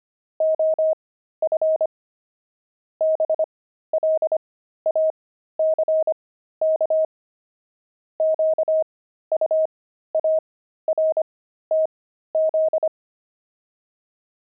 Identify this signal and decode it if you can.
Morse code, heard as a keyed tone, about 25 wpm, spaced as for 13 wpm, 630 Hz. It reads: OF BLACK QUARTZ
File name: pangram-fs-13-cs-25-love